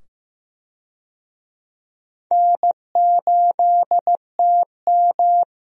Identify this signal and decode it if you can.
Morse code, heard as a keyed tone, about 15 wpm, 700 Hz. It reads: N8TM
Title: USA Call Signs - 4 Characters - 15wpm